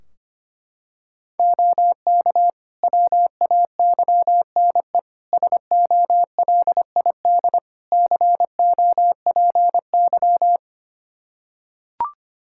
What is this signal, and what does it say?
Morse code, heard as a keyed tone, about 25 wpm, 700 Hz. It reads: OK WAYNE SOLID COPY
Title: Practice QSO - Exchange 3 - More Info - 25wpm